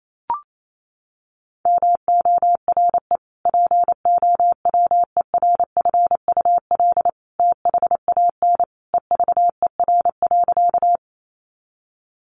Morse code, heard as a keyed tone, about 28 wpm, 700 Hz. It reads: MORE POWERFUL THAN EVER.
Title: Sentences from Top 1300 Words 28wpm